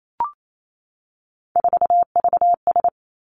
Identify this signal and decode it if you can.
Morse code, heard as a keyed tone, about 28 wpm, 700 Hz. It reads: 4VS
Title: Three Challenging Letter-Number Combinations 28wpm